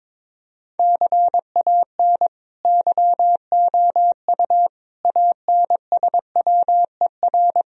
Morse code, heard as a keyed tone, about 22 wpm, 700 Hz. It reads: CAN YOU ANSWER
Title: Sentences from Top 1000 Words - Review Part 3 - 22wpm